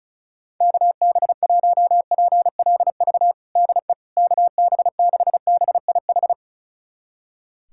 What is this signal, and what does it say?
Morse code, heard as a keyed tone, about 34 wpm, 700 Hz. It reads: KB1PLV DE KB6BIH
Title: USA Callsign DE Callsign - 6 Characters - 35wpm